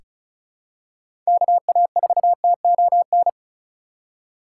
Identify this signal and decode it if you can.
Morse code, heard as a keyed tone, about 34 wpm, 700 Hz. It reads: KA4TON
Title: USA Call Signs - 6 Characters - 4x Character Spacing - 35wpm